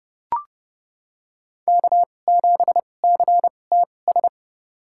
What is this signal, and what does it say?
Morse code, heard as a keyed tone, about 30 wpm, spaced as for 22 wpm, 700 Hz. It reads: K7CTS
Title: USA Call Signs - 5 Characters - 2x Character Spacing - 30wpm